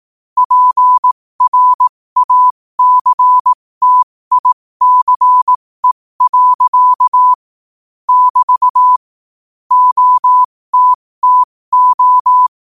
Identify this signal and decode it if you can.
Morse code, heard as a keyed tone, about 18 wpm, spaced as for 15 wpm, 1000 Hz. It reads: PRACTICE. <BT> OTTO
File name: qotdfiles / 15wpm